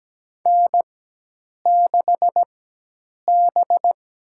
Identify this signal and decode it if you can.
Morse code, heard as a keyed tone, about 17 wpm, spaced as for 8 wpm, 700 Hz. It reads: N6B
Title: USA Call Signs - 4x Character Spacing - 17wpm